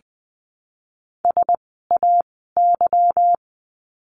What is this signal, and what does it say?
Morse code, heard as a keyed tone, about 20 wpm, 700 Hz. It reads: S A Y
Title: 206 Lesson 33f - Letter j - Words 20wpm